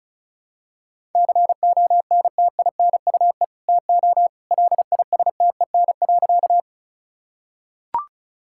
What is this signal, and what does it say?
Morse code, heard as a keyed tone, about 34 wpm, 700 Hz. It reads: CONTINUE TO LISTEN.
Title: Sentences from Top 1000 Words - Review Part 6 - 35wpm